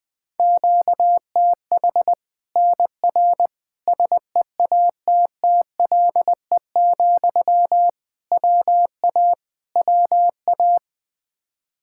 Morse code, heard as a keyed tone, about 20 wpm, 700 Hz. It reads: QTH NR SEATTLE, WA WA
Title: Practice QSO - Exchange 1 - Beginning - 20wpm